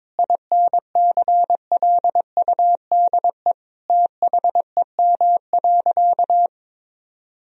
Morse code, heard as a keyed tone, about 21 wpm, 700 Hz. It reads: INCLUDE THEM.
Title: Sentences from Top 1000 Words - Review Part 3 - 22wpm